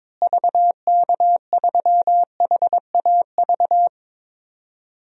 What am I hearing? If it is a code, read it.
Morse code, heard as a keyed tone, about 22 wpm, 700 Hz. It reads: VK3HAV
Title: DXCC-100 Call Signs - 6 Characters - Repeated 2x - 22wpm